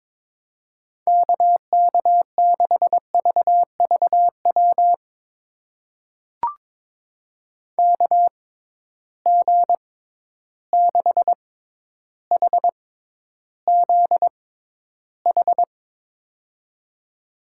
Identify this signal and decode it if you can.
Morse code, heard as a keyed tone, about 21 wpm, spaced as for 11 wpm, 700 Hz. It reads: KK6VVW KG6HZH
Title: USA Call Signs - 6 Characters - 6x Character Spacing - 22wpm